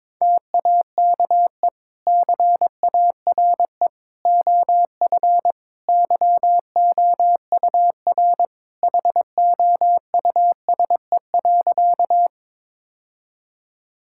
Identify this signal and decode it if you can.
Morse code, heard as a keyed tone, about 22 wpm, 700 Hz. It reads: TAKE CARE OF YOUR HOUSE.
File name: practice / 22wpm / Sentences with 2-Word Phrases - 22wpm